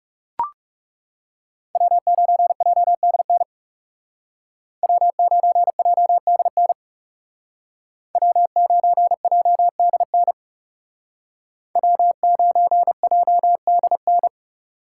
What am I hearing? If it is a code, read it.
Morse code, heard as a keyed tone, about 38 wpm, 700 Hz. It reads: W9JDN W9JDN W9JDN W9JDN